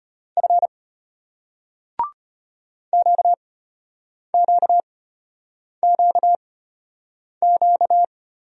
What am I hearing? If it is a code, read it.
Morse code, heard as a keyed tone, about 34 wpm, spaced as for 8 wpm, 700 Hz. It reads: F QQQQ